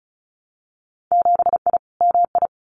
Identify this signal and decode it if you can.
Morse code, heard as a keyed tone, about 36 wpm, 700 Hz. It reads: 7I MI
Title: Winter Field Day Exchanges Encore 35wpm